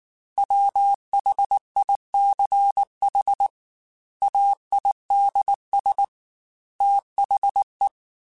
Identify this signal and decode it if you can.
Morse code, heard as a keyed tone, about 19 wpm, 800 Hz. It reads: WHICH AIDS THE